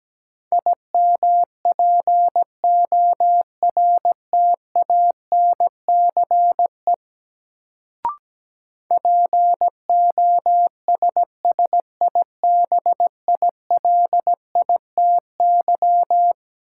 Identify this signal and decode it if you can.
Morse code, heard as a keyed tone, about 17 wpm, 700 Hz. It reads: IMPORTANCE POSSIBILITY
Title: Sets of 3 Words - New Word Drill - 17wpm